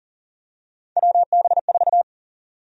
Morse code, heard as a keyed tone, about 40 wpm, 700 Hz. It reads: WB4
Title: USA Call Sign Prefixes - 40wpm